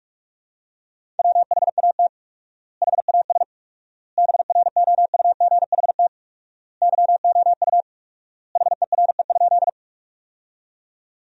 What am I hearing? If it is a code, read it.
Morse code, heard as a keyed tone, about 47 wpm, 700 Hz. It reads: WHAT HAS BROUGHT YOU HERE?
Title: 4x Word-Spacing - Sentences from Top 500 Words - 45wpm